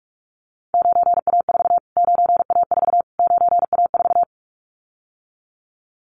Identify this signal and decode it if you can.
Morse code, heard as a keyed tone, about 44 wpm, 700 Hz. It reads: 9A4 9A4 9A4
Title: Callsign Prefixes for Top 25 DXCC - Repeated 3x - 45wpm